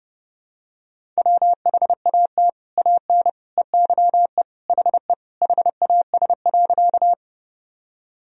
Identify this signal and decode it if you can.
Morse code, heard as a keyed tone, about 30 wpm, 700 Hz. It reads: WHAT AN EYE HE HAS.
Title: Sentences from Top 1000 Words - Review Part 1 - 30wpm